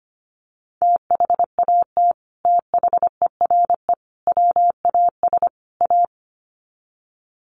Morse code, heard as a keyed tone, about 25 wpm, 700 Hz. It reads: THAT THERE WAS A
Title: Sets of 4 Words 25wpm